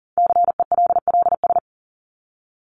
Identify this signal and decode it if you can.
Morse code, heard as a keyed tone, about 40 wpm, 700 Hz. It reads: CELLS
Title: ES:S Suffix - Top 100 Words with Base Words 40wpm